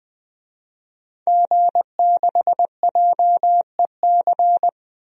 Morse code, heard as a keyed tone, about 20 wpm, 700 Hz. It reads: G6JEC
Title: DXCC-100 Call Signs - 5 Characters - Repeated 2x - 20wpm